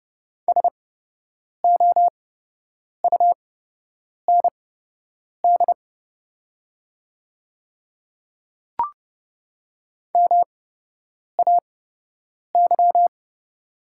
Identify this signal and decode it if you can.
Morse code, heard as a keyed tone, about 30 wpm, spaced as for 8 wpm, 700 Hz. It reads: SOUND MAY
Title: 8x Character Spacing - Top 200 Words - 30wpm